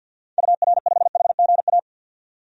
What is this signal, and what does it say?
Morse code, heard as a keyed tone, about 52 wpm, 700 Hz. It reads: UR5HGA
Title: DXCC-100 Call Signs - 6 Characters - 50wpm